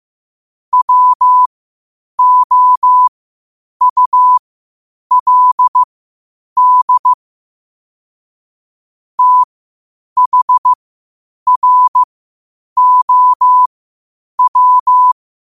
Morse code, heard as a keyed tone, about 15 wpm, spaced as for 8 wpm, 1000 Hz. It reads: WOULD THROW
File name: qotdfiles / 8wpm